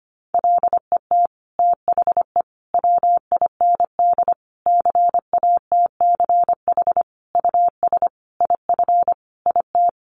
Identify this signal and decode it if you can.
Morse code, heard as a keyed tone, about 25 wpm, 700 Hz. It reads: LET THE WIND CATCH US IF IT
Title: Sentences from Top 1500 Words 25wpm